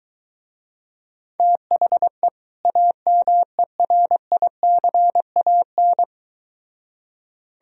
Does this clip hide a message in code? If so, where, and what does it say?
Yes: Morse code from 1.4 to 6.0 s, heard as a keyed tone, about 23 wpm, 700 Hz. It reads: THE AMERICAN